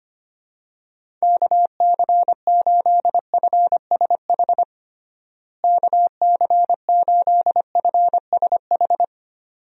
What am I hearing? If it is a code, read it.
Morse code, heard as a keyed tone, about 25 wpm, 700 Hz. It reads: KC8FSH KC8FSH
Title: USA Call Signs - 6 Characters - Repeated 2x - 25wpm